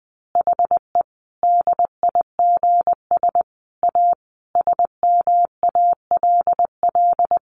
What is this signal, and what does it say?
Morse code, heard as a keyed tone, about 20 wpm, 700 Hz. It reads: HE DIGS A SMALL
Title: Easy Phrases 20wpm